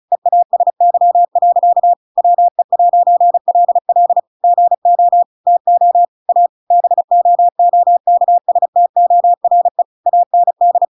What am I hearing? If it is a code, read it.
Morse code, heard as a keyed tone, about 34 wpm, 700 Hz. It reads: EASY. WE'LL GO TO A BOOKSTORE AND
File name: Pinocchio / Pinocchio ch8 35 35 ik1hge